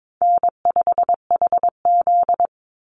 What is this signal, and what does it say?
Morse code, heard as a keyed tone, about 22 wpm, 700 Hz. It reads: N5HZ
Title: Call Signs - 22wpm